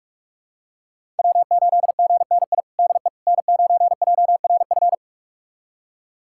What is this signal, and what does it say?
Morse code, heard as a keyed tone, about 46 wpm, 700 Hz. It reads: W8GNI DE N9JRF
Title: USA Callsign DE Callsign - 5 Characters - 45wpm